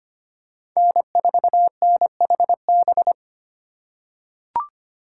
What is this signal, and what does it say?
Morse code, heard as a keyed tone, about 25 wpm, 700 Hz. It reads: N4NHB
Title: USA Call Signs - 5 Characters - 4x Character Spacing - 25wpm